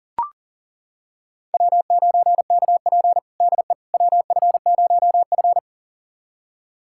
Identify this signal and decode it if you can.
Morse code, heard as a keyed tone, about 37 wpm, 700 Hz. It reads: W9KP DE WF0F
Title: USA Callsign DE Callsign - 4 Characters - 40wpm